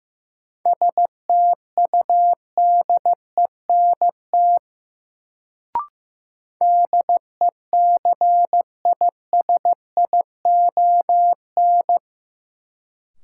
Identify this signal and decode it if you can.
Morse code, heard as a keyed tone, about 15 wpm, 700 Hz. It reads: STUDENT DECISION
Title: Sentences from Top 1000 Words - New Word Drill - 15wpm